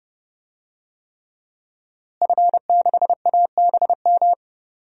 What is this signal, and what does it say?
Morse code, heard as a keyed tone, about 30 wpm, 700 Hz. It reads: F6ABM
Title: DXCC-100 Call Signs - 5 Characters - Repeated 3x - 30wpm